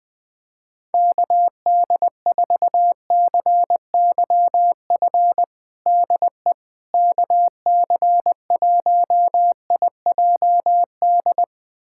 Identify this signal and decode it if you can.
Morse code, heard as a keyed tone, about 20 wpm, 700 Hz. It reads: KD4CYF DE KC1IJD
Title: USA Callsign DE Callsign - 6 Characters - 20wpm